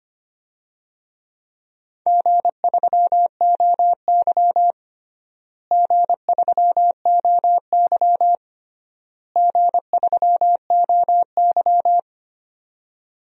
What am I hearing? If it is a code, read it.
Morse code, heard as a keyed tone, about 25 wpm, 700 Hz. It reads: G3OY G3OY G3OY
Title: DXCC-100 Call Signs - 4 Characters - Repeated 3x - 25wpm